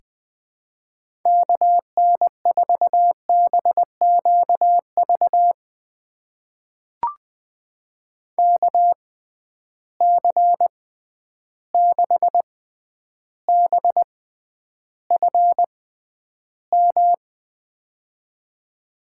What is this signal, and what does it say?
Morse code, heard as a keyed tone, about 20 wpm, spaced as for 10 wpm, 700 Hz. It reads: KN4BQV KC6BFM
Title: USA Call Signs - 6 Characters - 6x Character Spacing - 20wpm